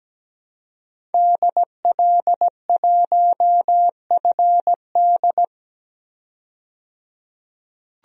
Morse code, heard as a keyed tone, about 17 wpm, 700 Hz. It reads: DL1FD
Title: DXCC-100 Call Signs - 5 Characters - 17wpm